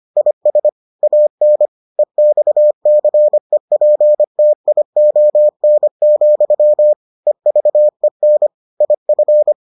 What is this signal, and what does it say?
Morse code, heard as a keyed tone, about 25 wpm, 600 Hz. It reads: IS AN EXCEPTION, EVEN IF